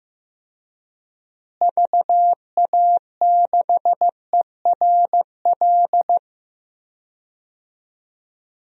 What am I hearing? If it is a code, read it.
Morse code, heard as a keyed tone, about 15 wpm, 700 Hz. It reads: VA6ERL
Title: DXCC-100 Call Signs - 6 Characters - Repeated 2x - 15wpm